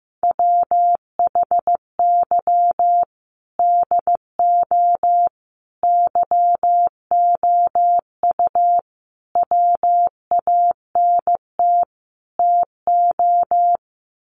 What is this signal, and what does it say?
Morse code, heard as a keyed tone, about 15 wpm, 700 Hz. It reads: WHY DO YOU WANT TO
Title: Sets of 5 Words Encore2 15wpm